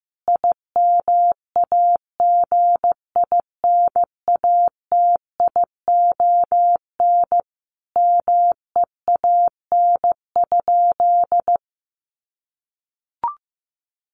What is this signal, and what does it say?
Morse code, heard as a keyed tone, about 15 wpm, 700 Hz. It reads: IMAGINATION MEAN?
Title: Sentences from Top 1300 Words 15wpm